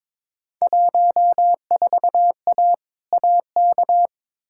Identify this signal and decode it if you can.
Morse code, heard as a keyed tone, about 22 wpm, 700 Hz. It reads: 14A AK
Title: Field Day Exchanges Encore - 22wpm